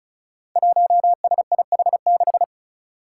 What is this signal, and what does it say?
Morse code, heard as a keyed tone, about 36 wpm, 700 Hz. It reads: 1SIH6
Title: Five Challenging Letter-Number Combinations 35wpm